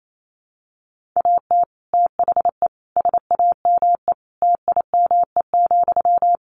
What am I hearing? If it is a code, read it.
Morse code, heard as a keyed tone, about 28 wpm, 700 Hz. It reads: AT THE SAME TIME,
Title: Sentences from Top 1500 Words 28wpm